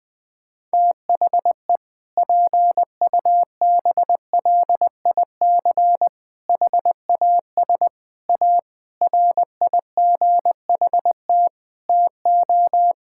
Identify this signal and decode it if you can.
Morse code, heard as a keyed tone, about 20 wpm, 700 Hz. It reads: THE PUBLIC HAS A RIGHT TO
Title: Sentences with 7-Word Phrases - 20wpm